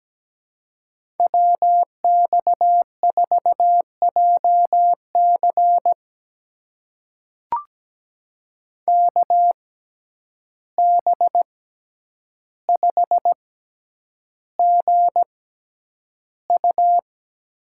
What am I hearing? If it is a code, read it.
Morse code, heard as a keyed tone, about 17 wpm, spaced as for 9 wpm, 700 Hz. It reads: WX4JC KB5GU
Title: USA Call Signs - 6x Character Spacing - 17wpm